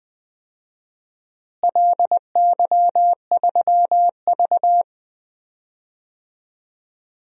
Morse code, heard as a keyed tone, about 20 wpm, 700 Hz. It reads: LY3V
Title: DXCC-100 Call Signs - 4 Characters - Repeated 2x - 20wpm